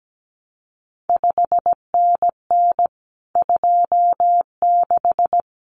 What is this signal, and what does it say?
Morse code, heard as a keyed tone, about 17 wpm, 700 Hz. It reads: 5NN 26
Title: CQ WW DX Contest Exchange Encore - 17wpm